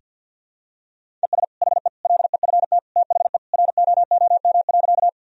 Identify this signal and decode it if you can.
Morse code, heard as a keyed tone, about 52 wpm, 700 Hz. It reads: ES HE LEFT THE ROOM.
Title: Sentences from Top 300 Words and Abbreviations 50wpm